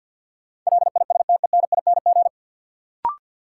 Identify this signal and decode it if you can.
Morse code, heard as a keyed tone, about 53 wpm, 700 Hz. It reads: LISTENING
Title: Sets of 5 Words - New Word Drill - 50wpm